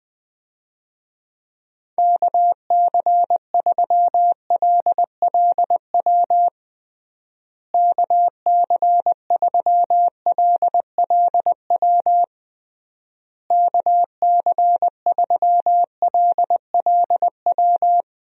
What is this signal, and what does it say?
Morse code, heard as a keyed tone, about 20 wpm, 700 Hz. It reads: KC3LLW KC3LLW KC3LLW
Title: USA Call Signs - 6 Characters - Repeated 3x - 20wpm